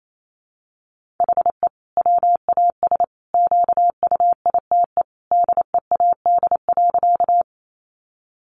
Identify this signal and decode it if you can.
Morse code, heard as a keyed tone, about 28 wpm, 700 Hz. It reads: HE WAS QUITE DEAD.
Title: Sentences from Top 1500 Words 28wpm